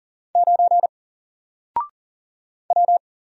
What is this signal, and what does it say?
Morse code, heard as a keyed tone, about 40 wpm, 700 Hz. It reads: 9 W
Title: Single Letter-Number Alt 40wpm